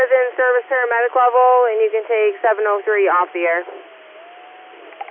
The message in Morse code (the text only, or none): none